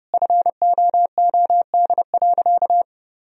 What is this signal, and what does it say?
Morse code, heard as a keyed tone, about 30 wpm, 700 Hz. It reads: FOOD.
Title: Sentences from Top 1000 Words - Review Part 1 - 30wpm